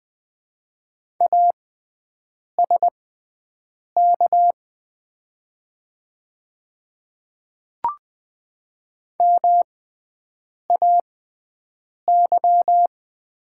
Morse code, heard as a keyed tone, about 20 wpm, spaced as for 7 wpm, 700 Hz. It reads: ASK MAY
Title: 6x Character Spacing - Top 200 Words - 20wpm